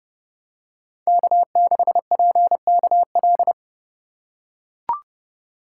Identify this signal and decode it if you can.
Morse code, heard as a keyed tone, about 30 wpm, 700 Hz. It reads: K6PKL